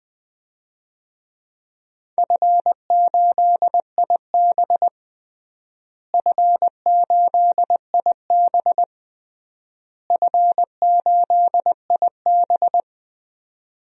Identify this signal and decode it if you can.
Morse code, heard as a keyed tone, about 20 wpm, 700 Hz. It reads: F8IB F8IB F8IB
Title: DXCC-100 Call Signs - 4 Characters - Repeated 3x - 20wpm